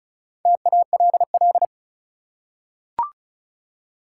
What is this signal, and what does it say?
Morse code, heard as a keyed tone, about 34 wpm, 700 Hz. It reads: TALL
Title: Four Letter Words - 35wpm